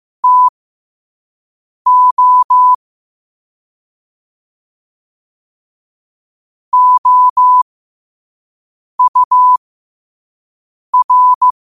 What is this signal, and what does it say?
Morse code, heard as a keyed tone, about 15 wpm, spaced as for 5 wpm, 1000 Hz. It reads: TO OUR
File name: qotdfiles / 5wpm